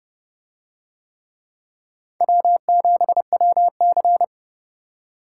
Morse code, heard as a keyed tone, about 29 wpm, 700 Hz. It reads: W7WC